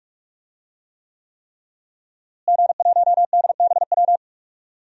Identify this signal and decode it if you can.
Morse code, heard as a keyed tone, about 46 wpm, 700 Hz. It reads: G1DBW
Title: DXCC-100 Call Signs - 5 Characters - Repeated 3x - 45wpm